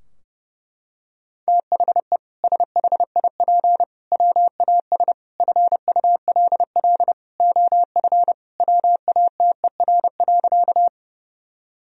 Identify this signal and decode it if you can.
Morse code, heard as a keyed tone, about 30 wpm, 700 Hz. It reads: THE SHIP WAS FULL OF WATER.
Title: Sentences from Top 1000 Words - Review Part 3 - 30wpm